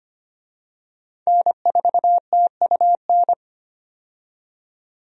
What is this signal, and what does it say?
Morse code, heard as a keyed tone, about 25 wpm, 700 Hz. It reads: N4TUN
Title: USA Call Signs - 5 Characters - 25wpm